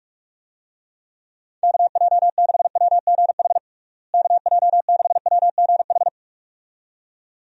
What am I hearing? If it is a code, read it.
Morse code, heard as a keyed tone, about 46 wpm, 700 Hz. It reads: KJ6WGH KJ6WGH
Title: USA Call Signs - 6 Characters - Repeated 2x - 45wpm